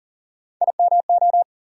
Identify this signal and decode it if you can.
Morse code, heard as a keyed tone, about 40 wpm, 700 Hz. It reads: IMO
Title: USA Call Sign Postfixes - 3 Letters 40wpm